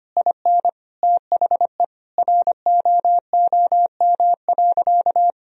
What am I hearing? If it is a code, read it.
Morse code, heard as a keyed tone, about 25 wpm, 700 Hz. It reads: IN THE ROOM.
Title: Sentences from Top 1000 Words - Review Part 3 - 25wpm